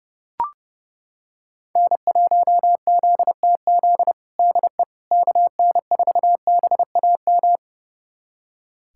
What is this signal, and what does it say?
Morse code, heard as a keyed tone, about 30 wpm, 700 Hz. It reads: N1ZTZ DE KN4BAM
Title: USA Callsign DE Callsign 30wpm